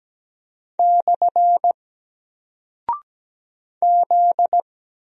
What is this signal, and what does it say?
Morse code, heard as a keyed tone, about 17 wpm, 700 Hz. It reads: / Z
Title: Single Character (Letter:Number:Punctuation:Prosign) Alt 17wpm